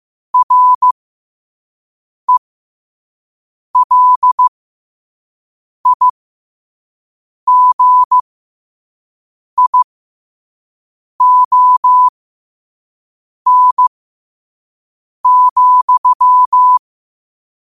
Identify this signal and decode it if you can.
Morse code, heard as a keyed tone, about 15 wpm, spaced as for 5 wpm, 1000 Hz. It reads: RELIGION,
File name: qotdfiles / 5wpm